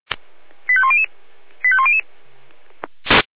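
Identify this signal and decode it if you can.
Five-tone selective call (ZVEI), heard as quick runs of five tones, tones of about 70 ms. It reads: 85100 85100
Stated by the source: ZVEI five-tone selective call received on an SDR.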